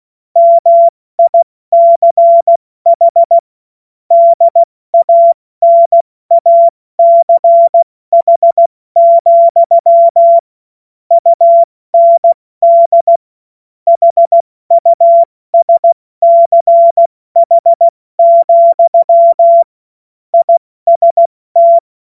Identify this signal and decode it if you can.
Morse code, heard as a keyed tone, about 16 wpm, 680 Hz. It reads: MICH DANACH, UND HUSCH, IST